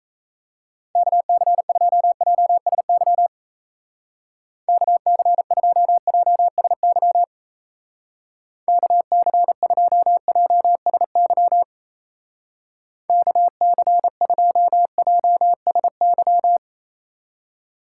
Morse code, heard as a keyed tone, about 34 wpm, 700 Hz. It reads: KC2JSY KC2JSY KC2JSY KC2JSY
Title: Speed Racing - USA Call Signs 28wpm